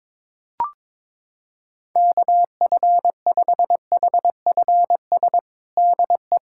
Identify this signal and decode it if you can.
Morse code, heard as a keyed tone, about 22 wpm, 700 Hz. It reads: KF5HFS DE
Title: USA Callsign DE Callsign - 6 Characters - 22wpm